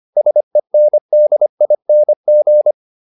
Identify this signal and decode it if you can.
Morse code, heard as a keyed tone, about 25 wpm, 600 Hz. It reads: SENDING